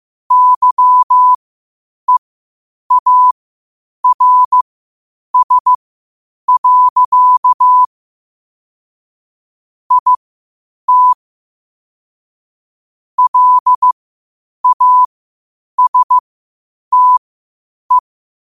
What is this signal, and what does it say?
Morse code, heard as a keyed tone, about 15 wpm, spaced as for 8 wpm, 1000 Hz. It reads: YEARS. IT LASTE
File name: qotdfiles / 8wpm